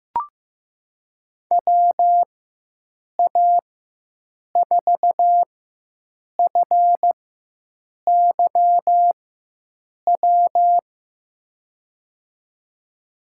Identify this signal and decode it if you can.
Morse code, heard as a keyed tone, about 15 wpm, spaced as for 7 wpm, 700 Hz. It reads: WA4FYW
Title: USA Call Signs - 6 Characters - 4x Character Spacing - 15wpm